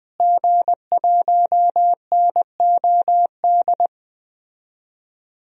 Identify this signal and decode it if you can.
Morse code, heard as a keyed tone, about 20 wpm, 700 Hz. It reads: G1NOD
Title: DXCC-100 Call Signs - 5 Characters - Repeated 2x - 20wpm